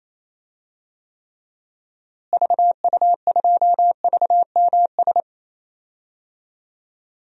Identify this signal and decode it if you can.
Morse code, heard as a keyed tone, about 28 wpm, 700 Hz. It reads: VU2VMS
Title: DXCC-100 Call Signs - 6 Characters - Repeated 2x - 28wpm